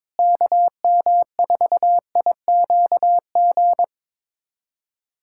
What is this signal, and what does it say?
Morse code, heard as a keyed tone, about 22 wpm, 700 Hz. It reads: KM4IQG
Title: USA Callsign DE Callsign - 6 Characters - 22wpm